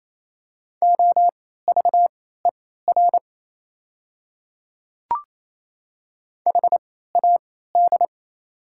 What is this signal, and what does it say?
Morse code, heard as a keyed tone, about 28 wpm, spaced as for 16 wpm, 690 Hz. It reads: OVER HAD
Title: 3x Character Spacing - Top 500 Words - 28wpm